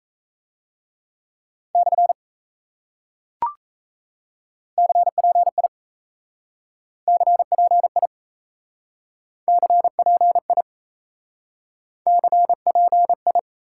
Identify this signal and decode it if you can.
Morse code, heard as a keyed tone, about 37 wpm, 700 Hz. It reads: / CPI CPI CPI CPI